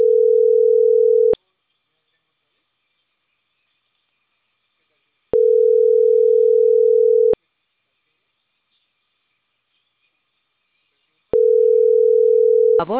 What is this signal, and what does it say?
Ringback tone (North America), 440 + 480 Hz.